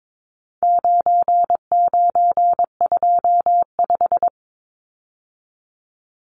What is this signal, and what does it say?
Morse code, heard as a keyed tone, about 22 wpm, 700 Hz. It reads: 9925
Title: Numbers - 4-Digit 22wpm